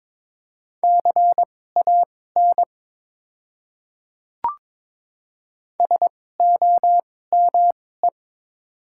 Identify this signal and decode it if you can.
Morse code, heard as a keyed tone, about 22 wpm, spaced as for 16 wpm, 700 Hz. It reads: CAN SOME
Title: 2x Character Spacing - Top 300 Words - 22wpm